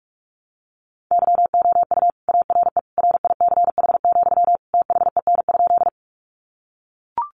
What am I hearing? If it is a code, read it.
Morse code, heard as a keyed tone, about 44 wpm, 700 Hz. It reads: YOU ARE RICH, THEN?
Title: Sentences from Top 1100 Words 45wpm